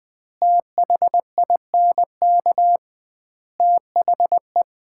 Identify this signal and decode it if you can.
Morse code, heard as a keyed tone, about 20 wpm, 700 Hz. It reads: THINK THE
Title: 2x Word-Spacing - Sets of 5 Words - 20wpm